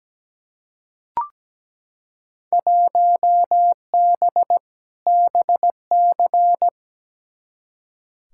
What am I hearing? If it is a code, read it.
Morse code, heard as a keyed tone, about 17 wpm, 700 Hz. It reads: 1B BC
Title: Field Day Exchanges - 17wpm